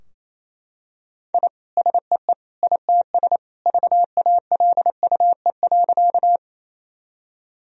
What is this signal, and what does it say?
Morse code, heard as a keyed tone, about 28 wpm, 700 Hz. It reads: I SEE ITS VALUE.
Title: Sentences from Top 1000 Words - Review Part 3 - 28wpm